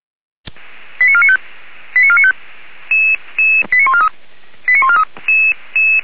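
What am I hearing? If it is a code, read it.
Five-tone selective call (ZVEI), heard as quick runs of five tones, tones of about 70 ms. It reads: 89495 89495 89143 89143
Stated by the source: ZVEI five-tone selective call 89155 received by SDR.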